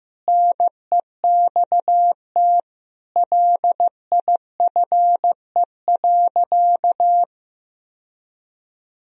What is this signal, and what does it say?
Morse code, heard as a keyed tone, about 15 wpm, 700 Hz. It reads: NEXT LIFE.
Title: Sentences from Top 1000 Words - Review Part 1 - 15wpm